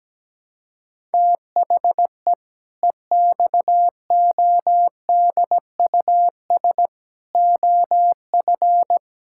Morse code, heard as a keyed tone, about 17 wpm, 700 Hz. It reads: THE EXODUS OF
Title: Pangram Challenge v2 - 17wpm